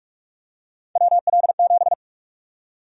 Morse code, heard as a keyed tone, about 46 wpm, 700 Hz. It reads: WL7
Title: USA Call Sign Prefixes - 45wpm